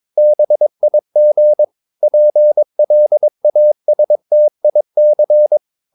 Morse code, heard as a keyed tone, about 22 wpm, 600 Hz. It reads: BIG PLASTIC
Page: Romanen Never af Ken Follet på engelsk.